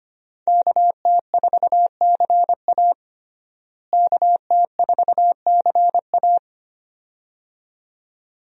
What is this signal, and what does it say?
Morse code, heard as a keyed tone, about 25 wpm, 700 Hz. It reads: KT4CA KT4CA
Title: USA Call Signs - 5 Characters - Repeated 2x - 25wpm